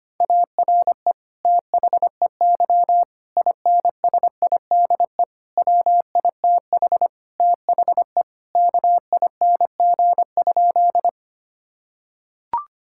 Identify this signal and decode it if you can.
Morse code, heard as a keyed tone, about 25 wpm, 700 Hz. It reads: ARE THEY INSIDE WITH THE KING?
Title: Sentences from Top 1000 Words - Review Part 6 - 25wpm